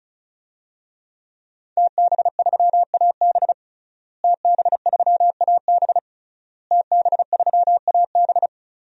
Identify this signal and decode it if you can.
Morse code, heard as a keyed tone, about 34 wpm, 700 Hz. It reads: TB3AB TB3AB TB3AB
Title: DXCC-100 Call Signs - 5 Characters - Repeated 3x - 35wpm